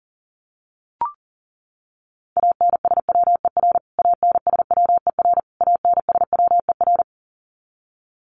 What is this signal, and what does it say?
Morse code, heard as a keyed tone, about 40 wpm, 700 Hz. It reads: ANSWER ANSWER ANSWER
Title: Top 500 Words Repeated 40wpm